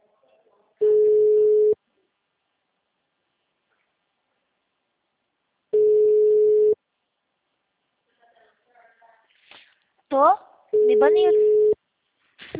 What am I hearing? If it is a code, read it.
Ringback tone (Europe, 425 Hz), with speech.
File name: out- -2024- - - .2578